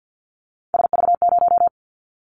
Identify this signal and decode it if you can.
Morse code, heard as a keyed tone, about 50 wpm, 700 Hz. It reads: SV0
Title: Callsign Prefixes for Top 50 DXCC - 50wpm